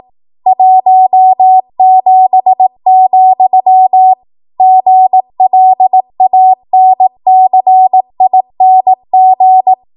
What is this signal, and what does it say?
Morse code, heard as a keyed tone, about 18 wpm, 750 Hz. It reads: 17, GLANCING
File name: Morse / 18WPM